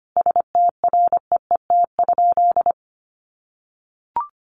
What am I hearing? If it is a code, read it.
Morse code, heard as a keyed tone, about 25 wpm, 700 Hz. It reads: STREET?
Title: Sentences from Top 900 Words 25wpm